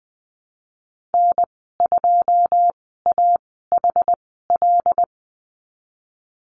Morse code, heard as a keyed tone, about 20 wpm, spaced as for 15 wpm, 700 Hz. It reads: N2AHL
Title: 161 Lesson 26g - Letter g - Call Signs 20wpm